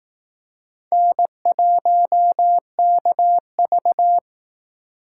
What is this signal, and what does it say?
Morse code, heard as a keyed tone, about 18 wpm, 700 Hz. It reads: N1KV
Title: Speed Racing - USA 1x2 Call Signs 15wpm